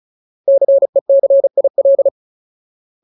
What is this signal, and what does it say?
Morse code, heard as a keyed tone, about 36 wpm, 550 Hz. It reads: CECIL
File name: Ultimate Word List / 35WPM Ultimate